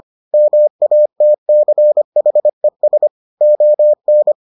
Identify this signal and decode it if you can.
Morse code, heard as a keyed tone, about 25 wpm, 600 Hz. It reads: MATCHES ON